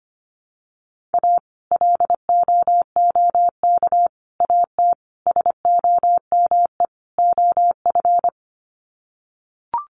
Morse code, heard as a keyed tone, about 25 wpm, 700 Hz. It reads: A LOOK AT SOME OF
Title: Sets of 5 Words Encore2 25wpm